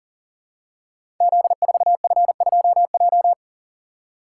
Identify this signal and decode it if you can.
Morse code, heard as a keyed tone, about 43 wpm, 700 Hz. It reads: Z4F2J
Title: Code Groups - 5 Letter:Numbers - 40wpm